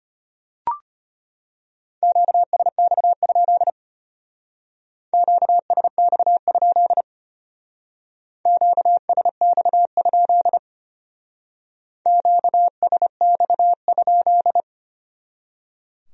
Morse code, heard as a keyed tone, about 31 wpm, 700 Hz. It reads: QSX? QSX? QSX? QSX?